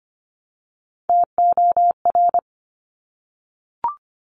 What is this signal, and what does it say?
Morse code, heard as a keyed tone, about 25 wpm, 700 Hz. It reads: TOR
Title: Top Three Letter Combinations in Words 25wpm